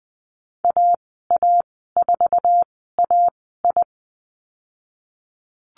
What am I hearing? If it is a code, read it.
Morse code, heard as a keyed tone, about 20 wpm, spaced as for 15 wpm, 700 Hz. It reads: AA4AI
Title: 069 Lesson 12g - Letter d - Call Signs 20wpm